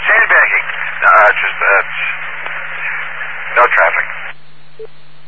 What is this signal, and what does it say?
Emergency Alert System (SAME) tones, heard as harsh buzzing bursts, with speech. It is heard from 0.5 to 4.4 s.